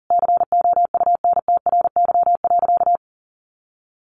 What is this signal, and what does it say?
Morse code, heard as a keyed tone, about 40 wpm, 700 Hz. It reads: COUNTRY.
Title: Sentences from Top 1400 Words 40wpm